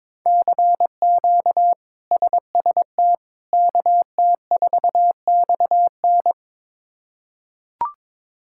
Morse code, heard as a keyed tone, about 22 wpm, 700 Hz. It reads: CQ SST KT4XN
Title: K1USN SST QSO Practice - 22wpm